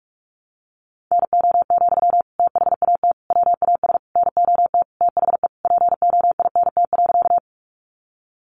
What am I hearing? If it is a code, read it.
Morse code, heard as a keyed tone, about 44 wpm, 700 Hz. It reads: NO, THAT WAS NOT THE POINT.
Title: Sentences from Top 600 Words 45wpm